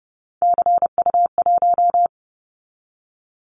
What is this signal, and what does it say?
Morse code, heard as a keyed tone, about 30 wpm, 700 Hz. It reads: CU1
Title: Callsign Prefixes for Top 100 DXCC - 30wpm